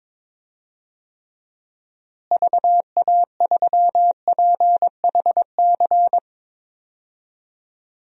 Morse code, heard as a keyed tone, about 22 wpm, 700 Hz. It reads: VA3PHC
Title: DXCC-100 Call Signs - 6 Characters - Repeated 3x - 22wpm